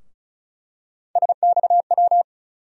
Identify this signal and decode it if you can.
Morse code, heard as a keyed tone, about 34 wpm, 700 Hz. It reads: SXW
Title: USA Call Sign Postfixes - 3 Letters 35wpm